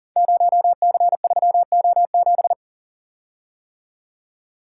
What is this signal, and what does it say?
Morse code, heard as a keyed tone, about 39 wpm, 690 Hz. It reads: 0C3O7
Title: Code Groups - 5 Letter:Numbers - 40wpm